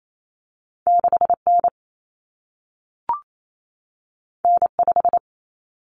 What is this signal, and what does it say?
Morse code, heard as a keyed tone, about 28 wpm, 700 Hz. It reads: 6N N5
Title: Common Cut Numbers - 2-Digit 28wpm